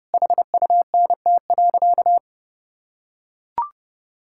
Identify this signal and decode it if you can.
Morse code, heard as a keyed tone, about 30 wpm, 700 Hz. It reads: HUNT.
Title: Sentences from Top 1600 Words 30wpm